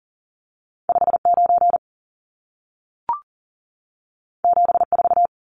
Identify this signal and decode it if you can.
Morse code, heard as a keyed tone, about 40 wpm, 700 Hz. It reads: 59 74
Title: Numbers - 2-Digit 40wpm